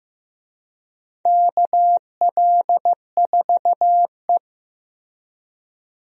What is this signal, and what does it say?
Morse code, heard as a keyed tone, about 15 wpm, 700 Hz. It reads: KL4E
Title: USA Call Signs - 4 Characters - Repeated 2x - 15wpm